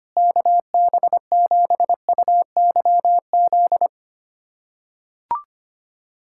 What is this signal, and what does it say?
Morse code, heard as a keyed tone, about 25 wpm, 700 Hz. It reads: KB7UYZ
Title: USA Callsign DE Callsign 25wpm